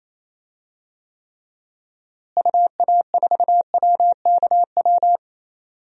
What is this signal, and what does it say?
Morse code, heard as a keyed tone, about 28 wpm, 700 Hz. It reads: UA4WKW
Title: DXCC-100 Call Signs - 6 Characters - Repeated 3x - 28wpm